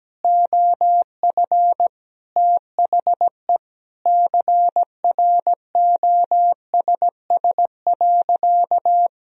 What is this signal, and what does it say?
Morse code, heard as a keyed tone, about 17 wpm, 700 Hz. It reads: OF THE CROSS.
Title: Sentences from Top 1000 Words - Review Part 1 - 17wpm